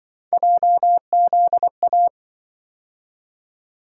Morse code, heard as a keyed tone, about 26 wpm, 700 Hz. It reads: JZA